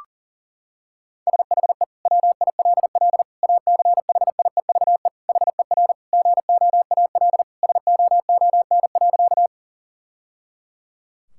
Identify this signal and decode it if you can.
Morse code, heard as a keyed tone, about 40 wpm, 700 Hz. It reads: SHE WILL ACHIEVE HER GOAL SOON.
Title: Sentences from Top 1000 Words - Review Part 6 - 40wpm